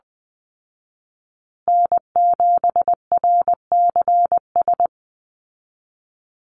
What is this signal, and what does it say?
Morse code, heard as a keyed tone, about 19 wpm, 700 Hz. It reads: N7RCS
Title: Call Signs Repeated 3x - 20wpm